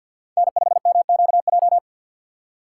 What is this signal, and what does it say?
Morse code, heard as a keyed tone, about 52 wpm, 690 Hz. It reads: N5MQJ
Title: USA Call Signs - 5 Characters - 6x Character Spacing - 50wpm